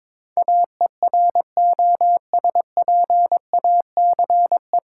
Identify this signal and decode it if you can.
Morse code, heard as a keyed tone, about 22 wpm, 700 Hz. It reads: AEROSPACE
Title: K1USN SST QSO Practice 22wpm